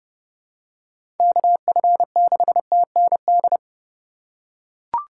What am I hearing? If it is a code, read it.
Morse code, heard as a keyed tone, about 30 wpm, 700 Hz. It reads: KF6TND
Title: USA Call Signs - 6 Characters - 3x Character Spacing - 30wpm